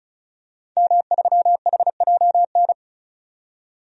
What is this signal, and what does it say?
Morse code, heard as a keyed tone, about 34 wpm, 700 Hz. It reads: M3HJN
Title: DXCC-100 Call Signs - 5 Characters - 35wpm